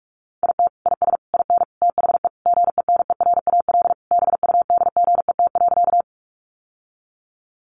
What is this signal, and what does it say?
Morse code, heard as a keyed tone, about 44 wpm, 700 Hz. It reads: IT IS IN THE GENERAL BUDGET.
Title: Sentences from Top 1200 Words 45wpm